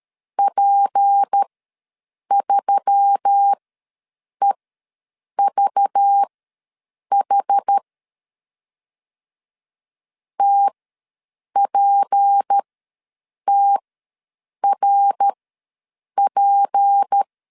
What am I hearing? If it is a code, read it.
Morse code, heard as a keyed tone, about 13 wpm, spaced as for 7 wpm, 800 Hz. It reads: P3EVH TPTRP
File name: LEZIONI / 6 PKX0 / 6 P ws 12wpm 3117